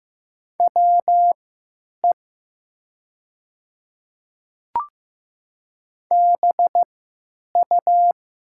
Morse code, heard as a keyed tone, about 15 wpm, spaced as for 9 wpm, 700 Hz. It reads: WE BU
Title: 3x Character Spacing - Top 100 Words - 15wpm